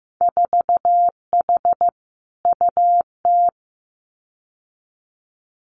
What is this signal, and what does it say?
Morse code, heard as a keyed tone, about 15 wpm, 700 Hz. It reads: 4H UT
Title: Winter Field Day Exchanges Encore 15wpm